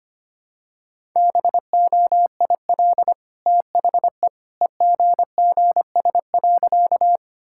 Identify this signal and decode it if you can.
Morse code, heard as a keyed tone, about 25 wpm, 700 Hz. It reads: BOIL THE EGGS.
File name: Easy Sentences with Hints - 25wpm